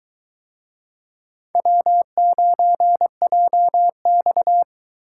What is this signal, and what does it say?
Morse code, heard as a keyed tone, about 23 wpm, 700 Hz. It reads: W9JX